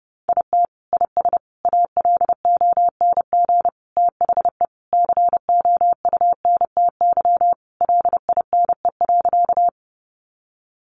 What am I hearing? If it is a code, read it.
Morse code, heard as a keyed tone, about 30 wpm, 700 Hz. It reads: IT IS ALONG THE COUNTY LINE.
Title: Sentences from Top 1200 Words 30wpm